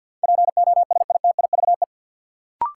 Morse code, heard as a keyed tone, about 51 wpm, 700 Hz. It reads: POSITIVE
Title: Sentences from Top 1000 Words - Review Part 5 - New Word Drill - 50wpm